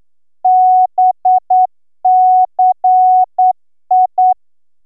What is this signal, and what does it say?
Morse code, heard as a keyed tone, about 10 wpm, 740 Hz. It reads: BCI